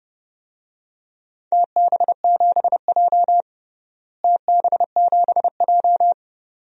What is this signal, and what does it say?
Morse code, heard as a keyed tone, about 32 wpm, 700 Hz. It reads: TB7J TB7J
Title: DXCC-100 Call Signs - 4 Characters - Repeated 2x - 30wpm